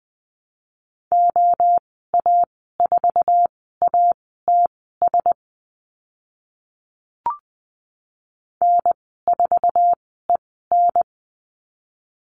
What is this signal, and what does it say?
Morse code, heard as a keyed tone, about 20 wpm, spaced as for 14 wpm, 700 Hz. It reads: OA4ATS N4EN
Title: 076 Lesson 13g - Letter L - Call Signs 20wpm